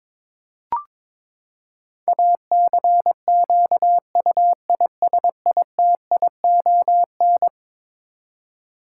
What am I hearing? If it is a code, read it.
Morse code, heard as a keyed tone, about 22 wpm, 700 Hz. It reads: ACQUISITION
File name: practice / 22wpm / Eleven Letter Words - 22wpm